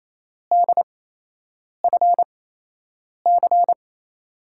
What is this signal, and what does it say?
Morse code, heard as a keyed tone, about 27 wpm, spaced as for 8 wpm, 700 Hz. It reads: DFC
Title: USA Call Signs - 6 Characters - 8x Character Spacing - 28wpm